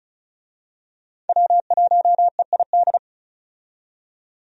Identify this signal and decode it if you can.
Morse code, heard as a keyed tone, about 34 wpm, 700 Hz. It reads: W1EID
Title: USA Call Signs - 5 Characters - 35wpm